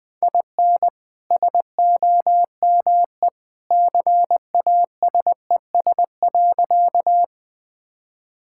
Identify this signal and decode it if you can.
Morse code, heard as a keyed tone, about 21 wpm, 700 Hz. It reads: IN SOME CASES.
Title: Sentences with 2-Word Phrases - Spoken Beginning - 20wpm